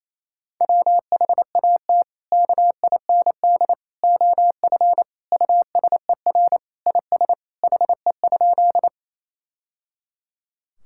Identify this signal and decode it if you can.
Morse code, heard as a keyed tone, about 28 wpm, 700 Hz. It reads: WHAT KIND OF USER IS HE?
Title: Sentences from Top 1000 Words - Review Part 5 - 28wpm